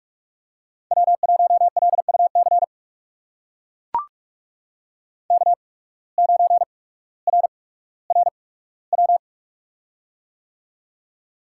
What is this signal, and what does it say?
Morse code, heard as a keyed tone, about 44 wpm, spaced as for 19 wpm, 700 Hz. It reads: W1LUC K9RRW
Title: USA Call Signs - 5 Characters - 8x Character Spacing - 45wpm